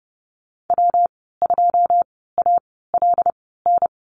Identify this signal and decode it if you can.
Morse code, heard as a keyed tone, about 31 wpm, spaced as for 17 wpm, 700 Hz. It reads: W2ALN
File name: Lesson 19g - Letter w - Call Signs 30wpm